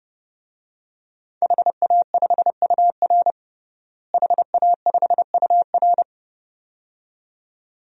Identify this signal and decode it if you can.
Morse code, heard as a keyed tone, about 30 wpm, 700 Hz. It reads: HA5UR HA5UR
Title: DXCC-100 Call Signs - 5 Characters - Repeated 2x - 30wpm